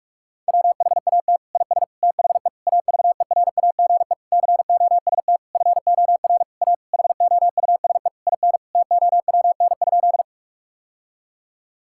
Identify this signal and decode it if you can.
Morse code, heard as a keyed tone, about 46 wpm, 700 Hz. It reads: WHAT IS THE AVERAGE COST FOR A HOUSE IN TOWN?
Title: Sentences from Top 1000 Words - Review Part 6 - 45wpm